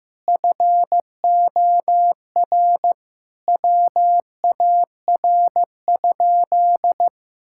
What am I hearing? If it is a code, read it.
Morse code, heard as a keyed tone, about 15 wpm, 700 Hz. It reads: FOR WAR?
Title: Sentences from Top 1000 Words - Review Part 5 - 15wpm